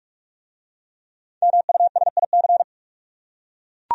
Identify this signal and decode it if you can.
Morse code, heard as a keyed tone, about 39 wpm, 700 Hz. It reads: MUSIC